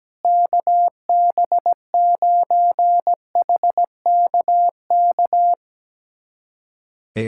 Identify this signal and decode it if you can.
Morse code, heard as a keyed tone, about 17 wpm, 700 Hz. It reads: KB9HKK
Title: USA Callsign DE Callsign 17wpm